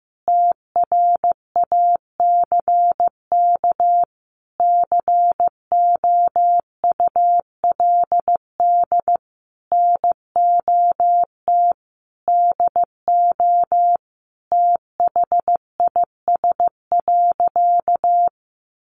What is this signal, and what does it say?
Morse code, heard as a keyed tone, about 15 wpm, 700 Hz. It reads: TRACK COULD NOT DO THIS.
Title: Sentences from Top 1100 Words 15wpm